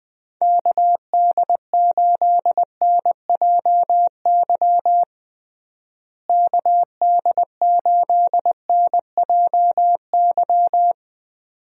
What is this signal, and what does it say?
Morse code, heard as a keyed tone, about 20 wpm, 700 Hz. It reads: KD8NJY KD8NJY
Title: USA Call Signs - 6 Characters - Repeated 2x - 20wpm